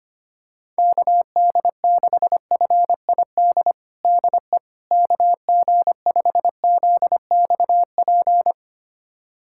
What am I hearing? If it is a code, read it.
Morse code, heard as a keyed tone, about 25 wpm, 700 Hz. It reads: KD6FID DE KG5ZXP
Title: USA Callsign DE Callsign - 6 Characters - 25wpm